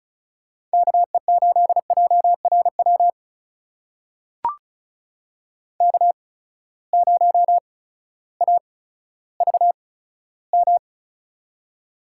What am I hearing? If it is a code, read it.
Morse code, heard as a keyed tone, about 36 wpm, spaced as for 16 wpm, 700 Hz. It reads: KE8JRW K0AVM
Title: USA Call Signs - 8x Character Spacing - 35wpm